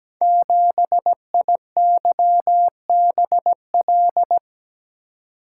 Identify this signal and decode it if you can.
Morse code, heard as a keyed tone, about 17 wpm, 700 Hz. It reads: 7IYBL
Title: Code Groups - 5 Letter:Numbers - 17wpm